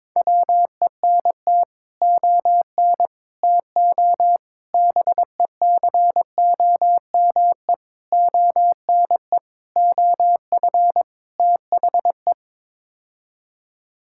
Morse code, heard as a keyed tone, about 22 wpm, 700 Hz. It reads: WENT ON TO BECOME ONE OF THE
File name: practice / 22wpm / Sets of 7 Words - 22wpm